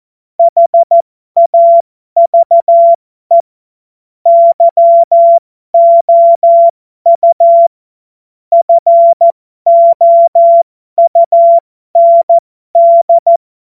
Morse code, heard as a keyed tone, about 14 wpm, spaced as for 12 wpm, 680 Hz. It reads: HAVE YOU FOUND